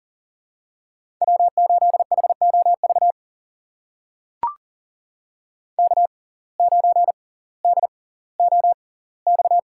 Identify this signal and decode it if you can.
Morse code, heard as a keyed tone, about 40 wpm, spaced as for 20 wpm, 700 Hz. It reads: W8HOV K9DOX
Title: USA Call Signs - 5 Characters - 6x Character Spacing - 40wpm